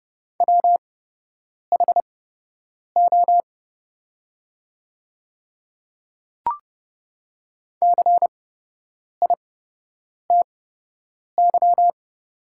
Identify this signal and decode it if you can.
Morse code, heard as a keyed tone, about 30 wpm, spaced as for 8 wpm, 700 Hz. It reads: WHO CITY
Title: 8x Character Spacing - Top 500 Words - 30wpm